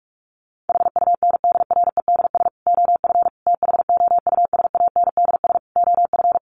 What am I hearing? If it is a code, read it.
Morse code, heard as a keyed tone, about 44 wpm, 700 Hz. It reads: HUNDREDS OF THOUSANDS OF
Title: Sets of 4 Words 45wpm